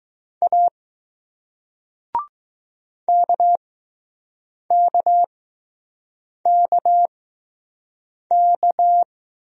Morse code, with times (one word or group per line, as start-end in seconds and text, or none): 0.4-0.7 A
3.1-9.0 KKKK